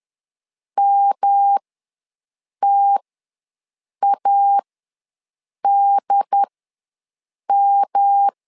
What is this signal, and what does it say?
Morse code, heard as a keyed tone, about 10 wpm, spaced as for 6 wpm, 800 Hz. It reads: MTADM
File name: LEZIONI / 3 MWRO / 3 M ws 10wpm 2121